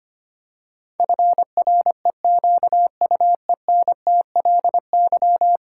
Morse code, heard as a keyed tone, about 25 wpm, 700 Hz. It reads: FREQUENTLY
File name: Ten Letter Words - 25wpm